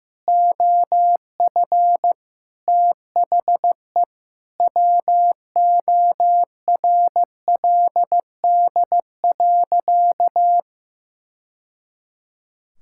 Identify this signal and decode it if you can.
Morse code, heard as a keyed tone, about 16 wpm, 700 Hz. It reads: OF THE WORLD.
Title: Sentences from Top 1000 Words - Review Part 5 - 15wpm